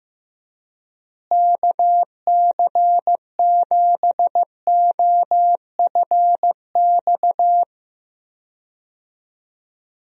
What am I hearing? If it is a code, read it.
Morse code, heard as a keyed tone, about 15 wpm, 700 Hz. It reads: KC7OFX
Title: USA Call Signs - 6 Characters - Repeated 3x - 15wpm